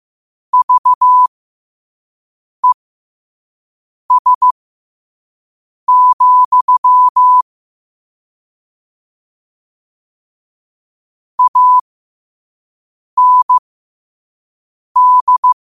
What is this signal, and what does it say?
Morse code, heard as a keyed tone, about 15 wpm, spaced as for 5 wpm, 1000 Hz. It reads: VES, AND
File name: qotdfiles / 5wpm